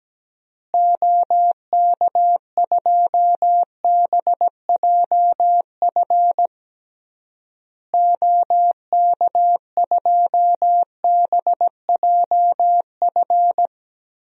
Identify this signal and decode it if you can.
Morse code, heard as a keyed tone, about 17 wpm, 700 Hz. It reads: OK2BJF OK2BJF
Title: DXCC-100 Call Signs - 6 Characters - Repeated 3x - 17wpm